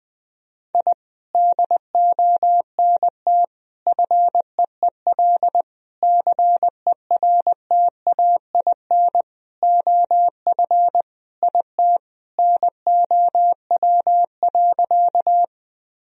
Morse code, heard as a keyed tone, about 20 wpm, 700 Hz. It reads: I DONT FEEL CERTAIN OF IT NOW.
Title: Sentences from Top 1000 Words - Review Part 2 - 20wpm